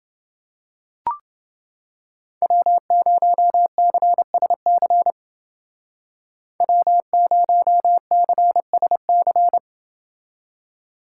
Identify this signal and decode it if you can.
Morse code, heard as a keyed tone, about 28 wpm, 700 Hz. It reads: W0CSC W0CSC